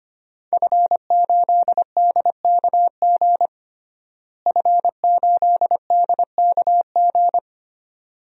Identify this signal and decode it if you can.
Morse code, heard as a keyed tone, about 25 wpm, 700 Hz. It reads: F8DKG F8DKG
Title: DXCC-100 Call Signs - 5 Characters - Repeated 2x - 25wpm